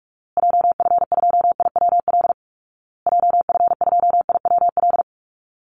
Morse code, heard as a keyed tone, about 44 wpm, 700 Hz. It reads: JF2IWL JF2IWL
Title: Call Signs Repeated 2x - 45wpm